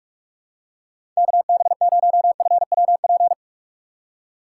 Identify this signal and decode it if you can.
Morse code, heard as a keyed tone, about 47 wpm, 700 Hz. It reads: KB0FWP
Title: USA Call Signs - 6 Characters - 4x Character Spacing - 45wpm